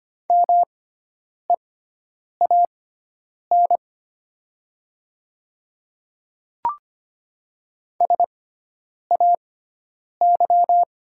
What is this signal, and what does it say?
Morse code, heard as a keyed tone, about 25 wpm, spaced as for 9 wpm, 690 Hz. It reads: MEAN SAY
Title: 6x Character Spacing - Top 300 Words - 25wpm